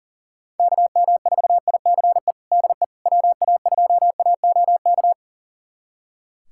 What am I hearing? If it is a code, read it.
Morse code, heard as a keyed tone, about 40 wpm, 700 Hz. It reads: KM4ICE DE WA2AOK
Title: USA Callsign DE Callsign - 6 Characters - 40wpm